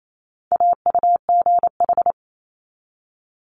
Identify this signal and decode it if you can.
Morse code, heard as a keyed tone, about 28 wpm, 700 Hz. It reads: AUGH
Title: Multi-letter Phonemes 28wpm